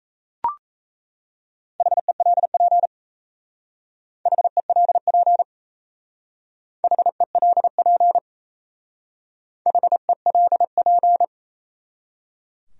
Morse code, heard as a keyed tone, about 32 wpm, 700 Hz. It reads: HELP HELP HELP HELP
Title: Speed-Racing Top 200 Words 28wpm